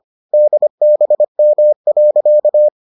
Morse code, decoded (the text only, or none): DBM.